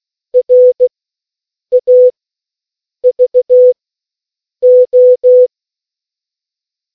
Morse code, heard as a keyed tone, about 16 wpm, 500 Hz. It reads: R A V O